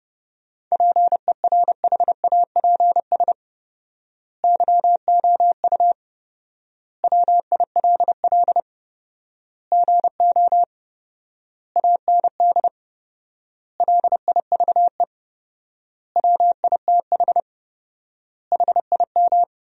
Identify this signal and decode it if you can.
Morse code, heard as a keyed tone, about 32 wpm, 700 Hz. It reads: PERHAPS YOU WILL GO AND LIVE WITH HIM
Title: 4x Word-Spacing - Sentences from Top 500 Words - 30wpm